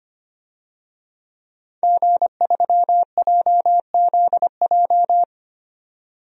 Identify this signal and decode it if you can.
Morse code, heard as a keyed tone, about 25 wpm, 700 Hz. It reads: G3JZJ
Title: DXCC-100 Call Signs - 5 Characters - 25wpm